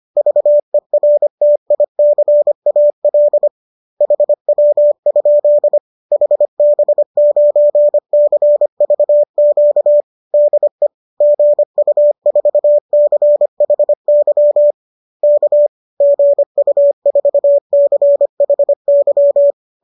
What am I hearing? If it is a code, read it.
Morse code, heard as a keyed tone, about 25 wpm, 590 Hz. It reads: VERTICAL HW? HB9CVQ DE GU4CHY K GU4CHY
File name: qso210 25